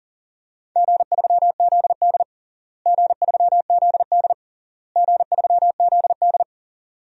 Morse code, heard as a keyed tone, about 40 wpm, 700 Hz. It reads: G3ZD G3ZD G3ZD
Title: DXCC-100 Call Signs - 4 Characters - Repeated 3x - 40wpm